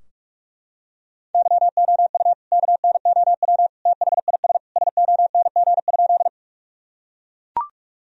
Morse code, heard as a keyed tone, about 42 wpm, 700 Hz. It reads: YOU KNOW THIS SONG?
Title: Sentences from Top 1000 Words - Review Part 2 - 45wpm